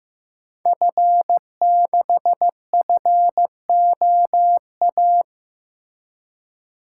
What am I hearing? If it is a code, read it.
Morse code, heard as a keyed tone, about 15 wpm, 700 Hz. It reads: F6FOA
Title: Code Groups - 5 Letter:Numbers - 15wpm